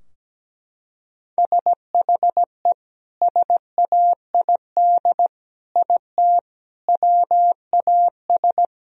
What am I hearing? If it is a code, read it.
Morse code, heard as a keyed tone, about 17 wpm, 700 Hz. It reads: SHE SAID IT WAS
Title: Sentences from Top 1000 Words - Review Part 2 - 17wpm